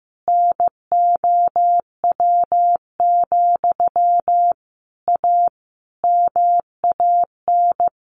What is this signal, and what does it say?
Morse code, heard as a keyed tone, about 15 wpm, 700 Hz. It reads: NOW, A MAN
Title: A Sound Of Thunder 15wpm